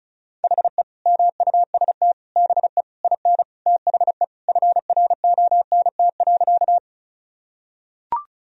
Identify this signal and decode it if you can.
Morse code, heard as a keyed tone, about 36 wpm, 700 Hz. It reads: HE MUST BE IN THE FRONT.
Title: Sentences from Top 1000 Words - Review Part 2 - 35wpm